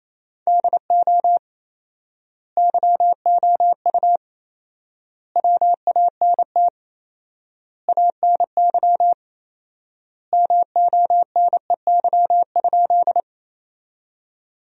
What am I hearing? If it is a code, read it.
Morse code, heard as a keyed tone, about 28 wpm, 700 Hz. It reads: DO YOU WANT ANY MONEY?
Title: 4x Word-Spacing - Sentences from Top 400 Words - 28wpm